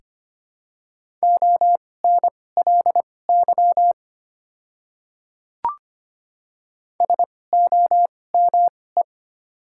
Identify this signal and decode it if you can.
Morse code, heard as a keyed tone, about 25 wpm, spaced as for 18 wpm, 690 Hz. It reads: ONLY SOME
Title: 2x Character Spacing - Top 500 Words - 25wpm